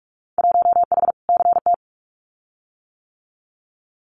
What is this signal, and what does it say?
Morse code, heard as a keyed tone, about 44 wpm, 700 Hz. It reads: 1H CT
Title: Winter Field Day Exchanges 45wpm